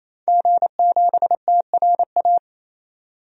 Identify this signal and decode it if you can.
Morse code, heard as a keyed tone, about 28 wpm, 700 Hz. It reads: G7TRA
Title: DXCC-100 Call Signs - 5 Characters - 28wpm